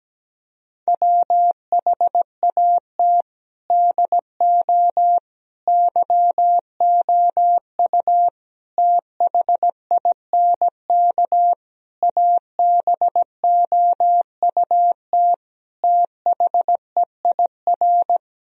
Morse code, heard as a keyed tone, about 17 wpm, 700 Hz. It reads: WHAT DO YOU THINK ABOUT THEIR
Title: Sentences from Top 1000 Words - Review Part 4 - 17wpm